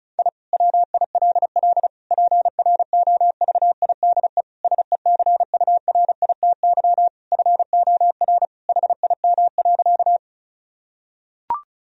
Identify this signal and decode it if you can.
Morse code, heard as a keyed tone, about 36 wpm, 700 Hz. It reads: I WILL PROVIDE SECURITY FOR HIM.
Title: Sentences from Top 1000 Words - Review Part 4 - 35wpm